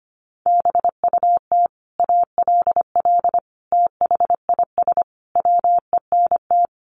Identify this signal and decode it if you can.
Morse code, heard as a keyed tone, about 25 wpm, 700 Hz. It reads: BUT ALL THIS WENT
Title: Sentences from Top 1500 Words 25wpm